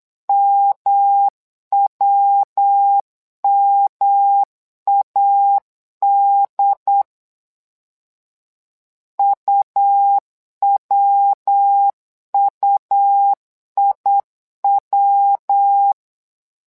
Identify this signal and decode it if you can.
Morse code, heard as a keyed tone, about 8 wpm, 800 Hz. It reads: MWMAD UWUIW
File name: LEZIONI / 3 MWRO / 3 MWRO 8wpm 2606